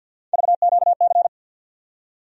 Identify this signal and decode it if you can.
Morse code, heard as a keyed tone, about 49 wpm, 700 Hz. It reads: VQC
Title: USA Call Sign Postfixes - 3 Letters 50wpm